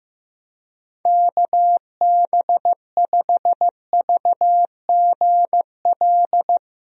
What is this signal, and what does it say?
Morse code, heard as a keyed tone, about 15 wpm, 700 Hz. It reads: KB5VGL
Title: USA Call Signs - 6 Characters - Repeated 3x - 15wpm